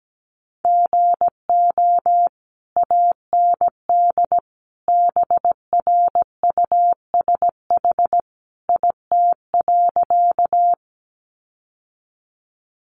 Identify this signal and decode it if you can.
Morse code, heard as a keyed tone, about 17 wpm, 700 Hz. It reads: GO AND BRUSH IT.
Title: Sentences from Top 1100 Words 17wpm